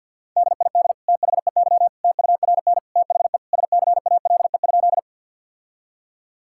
Morse code, heard as a keyed tone, about 50 wpm, 700 Hz. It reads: DID THEY TURN THE SCALE?
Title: Sentences from Top 1000 Words - Review Part 5 - 50wpm